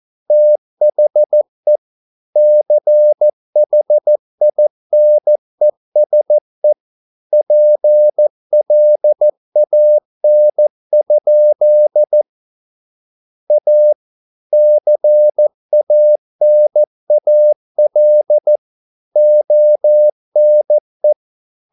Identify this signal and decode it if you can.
Morse code, heard as a keyed tone, about 14 wpm, 600 Hz. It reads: THE CHINESE PLAN? A CANAL ONE